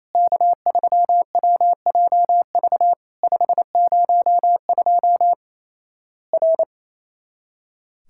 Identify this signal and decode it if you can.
Morse code, heard as a keyed tone, about 28 wpm, 700 Hz. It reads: K3WJV 502 R
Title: Worked All Europe DX Contest - QTC Exchange - 28wpm